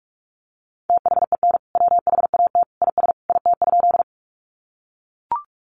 Morse code, heard as a keyed tone, about 44 wpm, 700 Hz. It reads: THEN WHAT IS IT?
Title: Sentences from Top 100 Words 45wpm